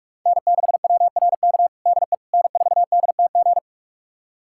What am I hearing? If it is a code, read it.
Morse code, heard as a keyed tone, about 46 wpm, 690 Hz. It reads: N6WRK DE N4DTG
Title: USA Callsign DE Callsign 45wpm